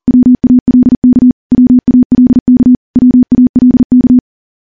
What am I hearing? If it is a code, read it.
Morse code, heard as a keyed tone, about 43 wpm, 250 Hz. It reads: WALK WALK WALK
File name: morse / morse-training-40WPM